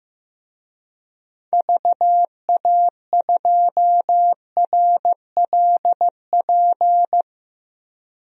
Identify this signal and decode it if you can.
Morse code, heard as a keyed tone, about 15 wpm, 690 Hz. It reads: VA2RLP
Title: DXCC-100 Call Signs - 6 Characters - 15wpm